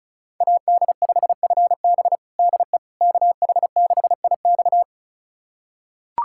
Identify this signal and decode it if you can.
Morse code, heard as a keyed tone, about 36 wpm, 700 Hz. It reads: AD5FB DE KH6IX
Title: USA Callsign DE Callsign - 5 Characters - 35wpm